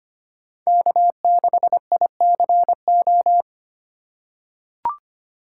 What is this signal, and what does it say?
Morse code, heard as a keyed tone, about 25 wpm, 700 Hz. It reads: K6ICO
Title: USA Call Signs - 5 Characters - 2x Character Spacing - 25wpm